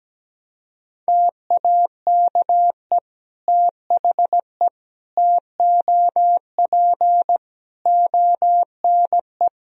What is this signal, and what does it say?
Morse code, heard as a keyed tone, about 17 wpm, 700 Hz. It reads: TAKE THE TOP ONE
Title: Sentences from Top 1000 Words - Review Part 2 - 17wpm